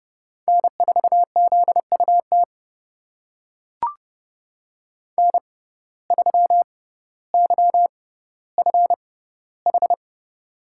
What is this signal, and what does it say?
Morse code, heard as a keyed tone, about 30 wpm, spaced as for 15 wpm, 700 Hz. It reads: N4ZUT N3YFH
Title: USA Call Signs - 5 Characters - 6x Character Spacing - 30wpm